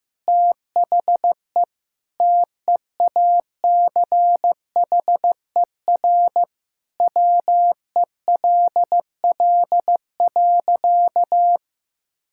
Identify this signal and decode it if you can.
Morse code, heard as a keyed tone, about 15 wpm, 700 Hz. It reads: THE TEACHER WELL.
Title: Sentences from Top 1000 Words - Review Part 6 - 15wpm